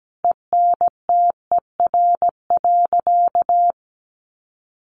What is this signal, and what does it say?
Morse code, heard as a keyed tone, about 17 wpm, 700 Hz. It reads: ENTER.
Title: Sentences from Top 1300 Words 17wpm